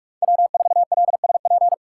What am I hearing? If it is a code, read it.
Morse code, heard as a keyed tone, about 46 wpm, 700 Hz. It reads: W4LSP
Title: USA Call Signs - 5 Characters - 3x Character Spacing - 45wpm